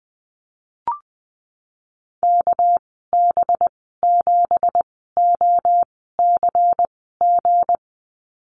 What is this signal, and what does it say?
Morse code, heard as a keyed tone, about 20 wpm, spaced as for 15 wpm, 700 Hz. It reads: KB7OCG
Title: 238 Lesson 38g - Letter q - Call Signs 20wpm